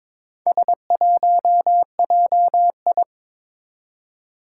Morse code, heard as a keyed tone, about 23 wpm, 700 Hz. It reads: S1JI
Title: Four Challenging Letter-Number Combinations 22wpm